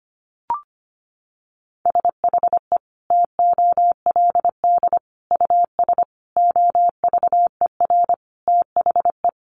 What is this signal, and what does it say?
Morse code, heard as a keyed tone, about 25 wpm, 700 Hz. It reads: SHE TOLD US OVER THE
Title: Sentences from Top 1100 Words 25wpm